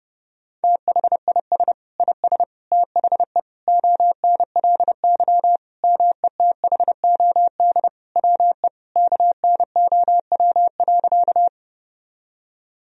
Morse code, heard as a keyed tone, about 30 wpm, 700 Hz. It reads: THIS IS THE ONLY METHOD WE KNOW.
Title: Sentences from Top 1000 Words - Review Part 3 - 30wpm